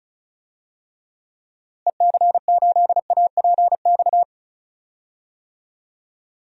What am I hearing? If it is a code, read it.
Morse code, heard as a keyed tone, about 36 wpm, 700 Hz. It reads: EC8APX
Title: DXCC-100 Call Signs - 6 Characters - Repeated 3x - 35wpm